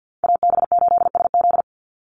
Morse code, heard as a keyed tone, about 50 wpm, 700 Hz. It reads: AB8SZ
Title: USA Call Signs 50wpm